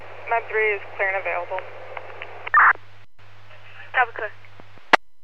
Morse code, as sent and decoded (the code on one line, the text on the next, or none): none